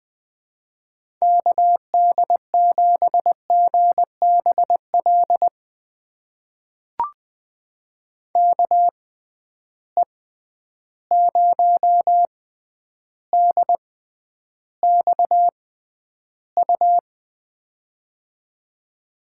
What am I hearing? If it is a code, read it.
Morse code, heard as a keyed tone, about 20 wpm, spaced as for 10 wpm, 700 Hz. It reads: KD7GBL KE0DXU
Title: USA Call Signs - 6 Characters - 6x Character Spacing - 20wpm